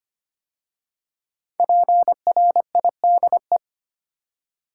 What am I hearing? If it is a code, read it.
Morse code, heard as a keyed tone, about 25 wpm, 700 Hz. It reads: PRIDE
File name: 5 Letter Words - First 3 Letters Given - 25wpm